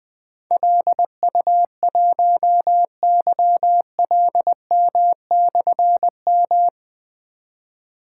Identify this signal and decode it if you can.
Morse code, heard as a keyed tone, about 20 wpm, 700 Hz. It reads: LU1YLM/M
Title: Difficult Call Signs 20wpm